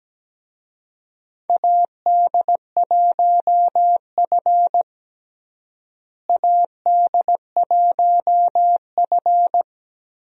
Morse code, heard as a keyed tone, about 17 wpm, 700 Hz. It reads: AD1F AD1F
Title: USA Call Signs - 4 Characters - Repeated 3x - 17wpm